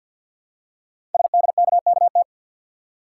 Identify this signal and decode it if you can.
Morse code, heard as a keyed tone, about 50 wpm, 700 Hz. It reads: SDKKT
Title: Code Groups - 5 Letters - 50wpm